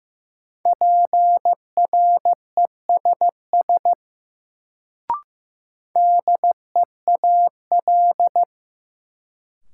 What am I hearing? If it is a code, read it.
Morse code, heard as a keyed tone, about 15 wpm, 700 Hz. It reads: PRESS DEAL
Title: Sentences from Top 800 Words - New Word Drill - 15wpm